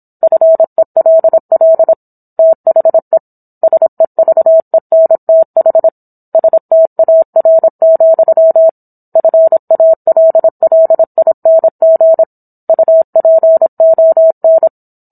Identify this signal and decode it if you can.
Morse code, heard as a keyed tone, about 27 wpm, 650 Hz. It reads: FELL THE SEVENTH STAR, FALLING UPON